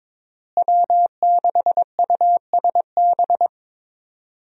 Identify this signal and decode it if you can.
Morse code, heard as a keyed tone, about 22 wpm, 700 Hz. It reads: W6USB
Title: USA Call Signs - 5 Characters - Repeated 2x - 22wpm